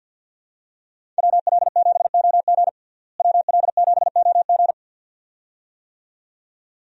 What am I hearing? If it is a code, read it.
Morse code, heard as a keyed tone, about 53 wpm, 700 Hz. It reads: WL7OG WL7OG
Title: USA Call Signs - 5 Characters - Repeated 2x - 50wpm